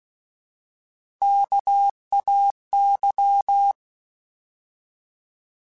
Morse code, heard as a keyed tone, about 16 wpm, 800 Hz. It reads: KAY
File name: Alphabetical / Alphabetical mp3s / Ks @15